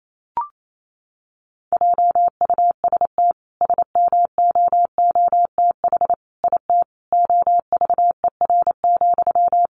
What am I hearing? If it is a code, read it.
Morse code, heard as a keyed tone, about 28 wpm, 700 Hz. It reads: JUST SMOOTH IT OVER,
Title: Sentences from Top 1400 Words 28wpm